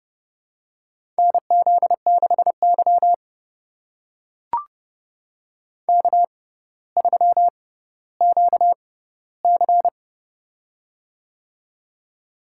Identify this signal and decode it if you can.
Morse code, heard as a keyed tone, about 28 wpm, spaced as for 15 wpm, 700 Hz. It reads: NZ6Y K3QC
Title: USA Call Signs - 4 Characters - 6x Character Spacing - 30wpm